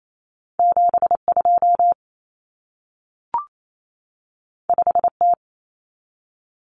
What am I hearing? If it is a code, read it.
Morse code, heard as a keyed tone, about 28 wpm, 700 Hz. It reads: 72 5T
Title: Common Cut Numbers - 2-Digit 28wpm